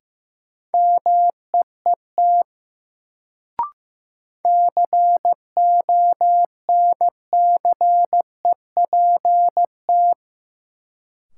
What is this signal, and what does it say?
Morse code, heard as a keyed tone, about 15 wpm, 700 Hz. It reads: MEET CONCEPT
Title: Sentences from Top 1100 Words - New Word Drill Alt - 15wpm